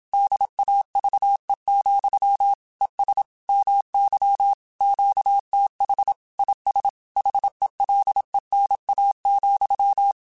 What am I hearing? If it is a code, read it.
Morse code, heard as a keyed tone, about 27 wpm, 800 Hz. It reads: DAVE, ES MY QTH IS HELENA,